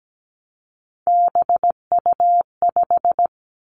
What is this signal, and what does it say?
Morse code, heard as a keyed tone, about 17 wpm, 700 Hz. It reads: BU5
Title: Three Challenging Letter-Number Combinations 17wpm